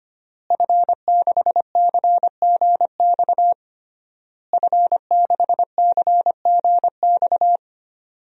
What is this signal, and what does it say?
Morse code, heard as a keyed tone, about 25 wpm, 700 Hz. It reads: F6CGX F6CGX
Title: DXCC-100 Call Signs - 5 Characters - Repeated 3x - 25wpm